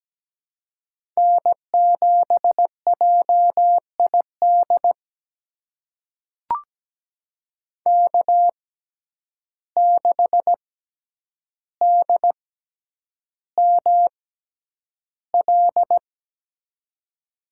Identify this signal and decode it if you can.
Morse code, heard as a keyed tone, about 17 wpm, spaced as for 9 wpm, 700 Hz. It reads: N7JID K6DML
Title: USA Call Signs - 5 Characters - 6x Character Spacing - 17wpm